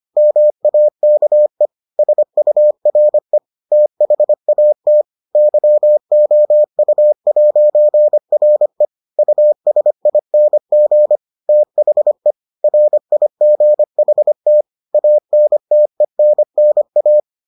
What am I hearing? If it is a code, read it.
Morse code, heard as a keyed tone, about 25 wpm, 600 Hz. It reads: MAKE SURE THAT YOU'RE USING THE RIGHT ANTENNA